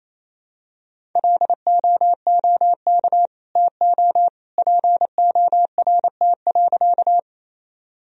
Morse code, heard as a keyed tone, about 28 wpm, 700 Hz. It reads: LOOK TO PORT.
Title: Sentences from Top 1000 Words - Review Part 1 - 28wpm